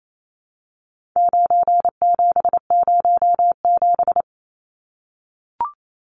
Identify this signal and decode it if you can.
Morse code, heard as a keyed tone, about 28 wpm, 700 Hz. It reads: 9707
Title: Numbers - 4-Digit 28wpm